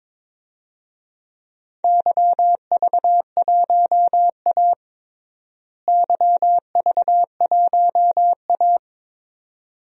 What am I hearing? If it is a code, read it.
Morse code, heard as a keyed tone, about 22 wpm, 700 Hz. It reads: YV1A YV1A
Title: DXCC-100 Call Signs - 4 Characters - Repeated 2x - 22wpm